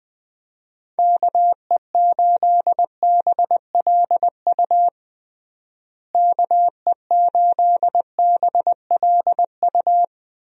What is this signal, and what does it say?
Morse code, heard as a keyed tone, about 20 wpm, 700 Hz. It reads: KE8BLU KE8BLU
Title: USA Call Signs - 6 Characters - Repeated 3x - 20wpm